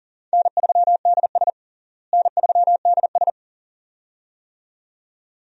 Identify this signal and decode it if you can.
Morse code, heard as a keyed tone, about 37 wpm, 700 Hz. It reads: N3DS N3DS
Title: USA Call Signs - 4 Characters - Repeated 2x - 40wpm